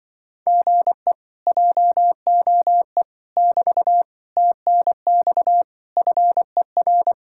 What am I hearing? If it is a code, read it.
Morse code, heard as a keyed tone, about 25 wpm, 700 Hz. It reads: GE JOE <BT> TNX FER